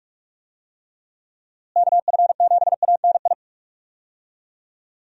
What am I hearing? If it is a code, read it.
Morse code, heard as a keyed tone, about 41 wpm, 700 Hz. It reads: KF7ANI
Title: USA Call Signs - 6 Characters - Repeated 2x - 45wpm